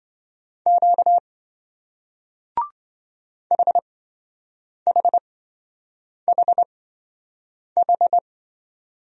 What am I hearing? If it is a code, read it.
Morse code, heard as a keyed tone, about 27 wpm, spaced as for 7 wpm, 700 Hz. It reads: Q HHHH